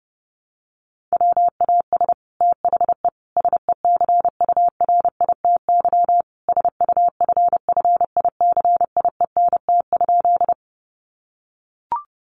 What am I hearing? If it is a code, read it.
Morse code, heard as a keyed tone, about 30 wpm, 700 Hz. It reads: WAS THE SECURITY SUFFICIENT?
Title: Sentences from Top 1400 Words 30wpm